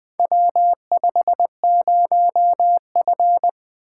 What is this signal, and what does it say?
Morse code, heard as a keyed tone, about 20 wpm, 700 Hz. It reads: W50F
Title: Four Letter-Number Combinations - 20wpm